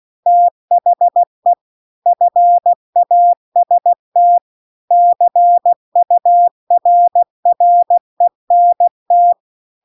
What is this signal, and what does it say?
Morse code, heard as a keyed tone, about 16 wpm, 700 Hz. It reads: THE FAST CURRENT
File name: CW-The-Big-Twoheaded-River-2-16WPM